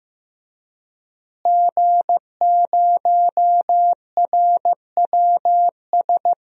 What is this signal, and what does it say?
Morse code, heard as a keyed tone, about 15 wpm, 700 Hz. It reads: G0RWS
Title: DXCC-100 Call Signs - 5 Characters - Repeated 2x - 15wpm